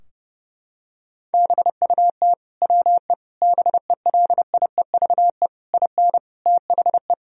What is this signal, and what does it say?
Morse code, heard as a keyed tone, about 30 wpm, 700 Hz. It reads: BUT WE BELIEVE IN THE
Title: Sentences from Top 1000 Words - Review Part 3 - 30wpm